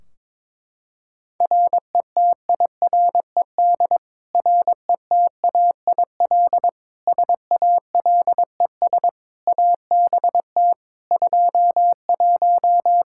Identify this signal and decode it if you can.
Morse code, heard as a keyed tone, about 22 wpm, 700 Hz. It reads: RETIRED RETAIL SALES ABT 21
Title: Practice QSO - Exchange 3 - More Info - 22wpm